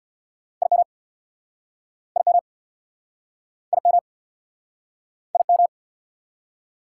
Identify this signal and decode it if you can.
Morse code, heard as a keyed tone, about 64 wpm, 700 Hz. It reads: IM IM IM IM